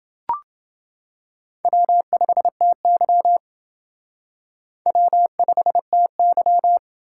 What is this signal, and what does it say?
Morse code, heard as a keyed tone, about 28 wpm, 700 Hz. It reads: W5TY W5TY